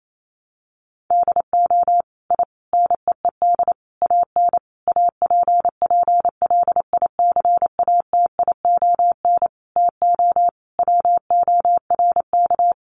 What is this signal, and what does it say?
Morse code, heard as a keyed tone, about 28 wpm, 700 Hz. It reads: DO I NEED AN APPLICATION TO WORK
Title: Sentences from Top 1200 Words 28wpm